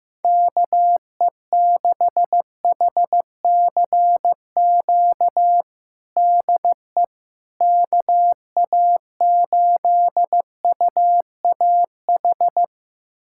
Morse code, heard as a keyed tone, about 16 wpm, 700 Hz. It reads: KE6HCQ DE KA8UAH
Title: USA Callsign DE Callsign 15wpm